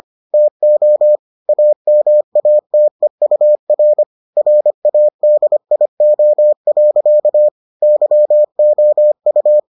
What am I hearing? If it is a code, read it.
Morse code, heard as a keyed tone, about 25 wpm, 600 Hz. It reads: TO AMATEUR RADIO. YOU